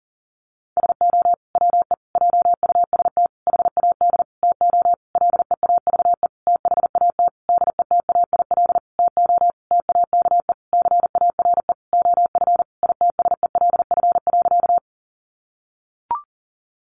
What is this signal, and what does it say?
Morse code, heard as a keyed tone, about 40 wpm, 700 Hz. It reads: SO WE JUST HAD TO LEAVE THAT DETAIL TO TAKE CARE OF ITSELF.
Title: Sentences from Top 1500 Words 40wpm